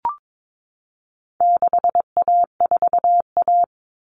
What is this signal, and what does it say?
Morse code, heard as a keyed tone, about 22 wpm, 700 Hz. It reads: 6A4A
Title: Common Cut Numbers - 4-Digit 22wpm